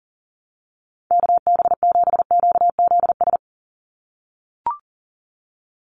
Morse code, heard as a keyed tone, about 40 wpm, 700 Hz. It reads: KB7QZS
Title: USA Call Signs 40wpm